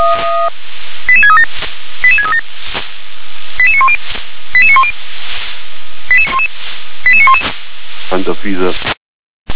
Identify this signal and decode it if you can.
Five-tone selective call (ZVEI), heard as quick runs of five tones, tones of about 70 ms, with speech. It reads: 88637 88637 88910 88910 88911